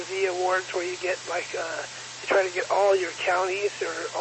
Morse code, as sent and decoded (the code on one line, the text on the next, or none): none